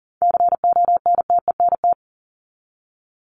Morse code, heard as a keyed tone, about 40 wpm, 700 Hz. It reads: CONTENT
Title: Top 751-1000 Words 40wpm